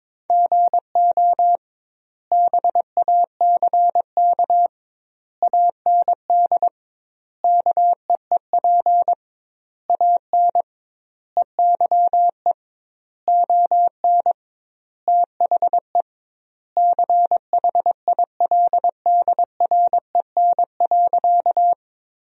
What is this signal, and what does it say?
Morse code, heard as a keyed tone, about 22 wpm, 700 Hz. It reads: GO BACK AND KEEP AN EYE ON THE CHILDREN.
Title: 2x Word-Spacing - Sentences from Top 300 Words - 22wpm